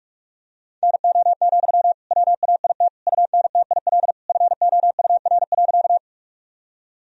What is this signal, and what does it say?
Morse code, heard as a keyed tone, about 45 wpm, 700 Hz. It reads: NO, WAIT UNTIL FOUR.
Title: Sentences from Top 1000 Words - Review Part 1 - 45wpm